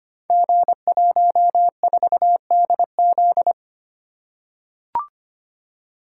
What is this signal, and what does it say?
Morse code, heard as a keyed tone, about 25 wpm, 700 Hz. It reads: G14DZ
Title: Code Groups - 5 Letter:Numbers - 25wpm